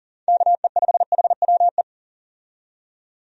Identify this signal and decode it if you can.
Morse code, heard as a keyed tone, about 40 wpm, 700 Hz. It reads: KE5HWE
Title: USA Call Signs - 6 Characters - 40wpm